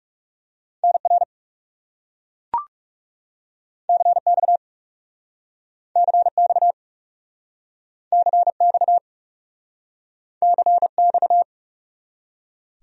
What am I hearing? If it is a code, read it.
Morse code, heard as a keyed tone, about 37 wpm, 700 Hz. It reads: NR CX CX CX CX
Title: Speed-Racing Abbreviations - 30wpm